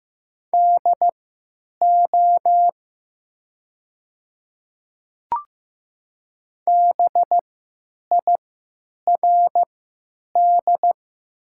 Morse code, heard as a keyed tone, about 15 wpm, spaced as for 8 wpm, 700 Hz. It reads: DO BIRD
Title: 3x Character Spacing - Top 500 Words - 15wpm